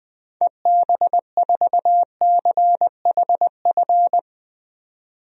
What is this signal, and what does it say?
Morse code, heard as a keyed tone, about 20 wpm, 700 Hz. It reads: EB4CHF
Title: DXCC-100 Call Signs - 6 Characters - 20wpm